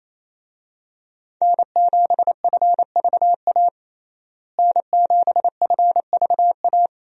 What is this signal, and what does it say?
Morse code, heard as a keyed tone, about 28 wpm, 700 Hz. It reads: N7FVA N7FVA
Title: USA Call Signs - 5 Characters - Repeated 2x - 28wpm